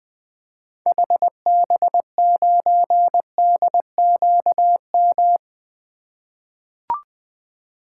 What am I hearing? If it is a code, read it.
Morse code, heard as a keyed tone, about 21 wpm, 700 Hz. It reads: HB9DQM
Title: 2023 SOTA Activator Call Signs - 20wpm